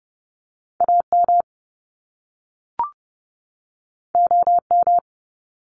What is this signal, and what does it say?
Morse code, heard as a keyed tone, about 30 wpm, 700 Hz. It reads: AM OM
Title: Top Two Letter Combinations in Words 30wpm